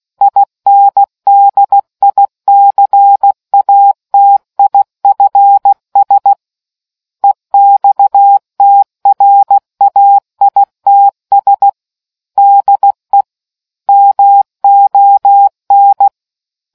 Morse code, heard as a keyed tone, about 16 wpm, 800 Hz. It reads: INDICATIFS EXTRAITS DE MON